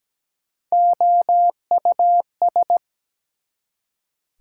Morse code, heard as a keyed tone, about 17 wpm, 700 Hz. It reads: OUS
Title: Prefixes and Suffixes 17wpm